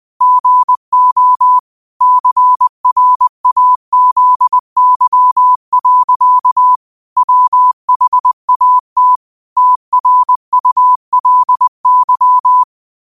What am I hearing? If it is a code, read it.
Morse code, heard as a keyed tone, about 20 wpm, 1000 Hz. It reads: GO CRAZY. WHAT TRULY
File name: qotdfiles / 20wpm